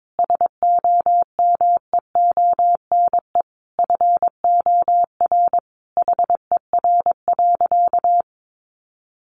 Morse code, heard as a keyed tone, about 22 wpm, 700 Hz. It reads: SOMEONE FOR HER.
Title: Sentences from Top 600 Words 22wpm